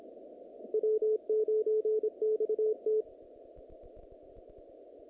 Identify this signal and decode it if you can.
Morse code, heard as a keyed tone, about 25 wpm, 430 Hz. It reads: W9XT
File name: contests / naqp-cw-k5tr-op-k5pi- -2200